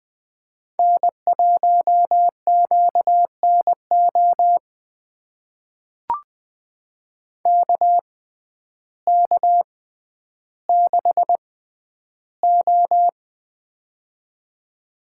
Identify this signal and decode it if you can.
Morse code, heard as a keyed tone, about 20 wpm, spaced as for 11 wpm, 700 Hz. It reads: N1QNO KK6O
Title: USA Call Signs - 6x Character Spacing - 20wpm